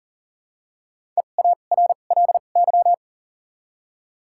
Morse code, heard as a keyed tone, about 40 wpm, 700 Hz. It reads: E A R L Y
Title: 2x Character Spacing - Top 500 Words - 40wpm